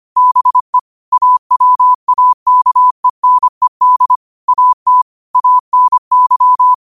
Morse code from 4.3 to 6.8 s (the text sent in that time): AT ANY